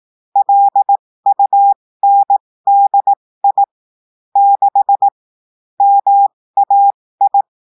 Morse code, heard as a keyed tone, about 18 wpm, spaced as for 15 wpm, 800 Hz. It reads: LUNDI 6 MAI
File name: cours-du-lundi files / cours-1580-texte-chiffre-15-mpm